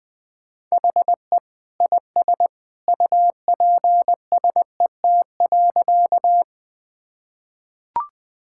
Morse code, heard as a keyed tone, about 20 wpm, 700 Hz. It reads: HE IS UPSET.
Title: Sentences from Top 1000 Words - Review Part 5 - 20wpm